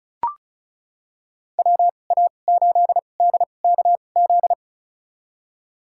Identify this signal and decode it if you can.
Morse code, heard as a keyed tone, about 36 wpm, spaced as for 25 wpm, 700 Hz. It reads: WA8DKZ
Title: USA Call Signs - 6 Characters - 2x Character Spacing - 35wpm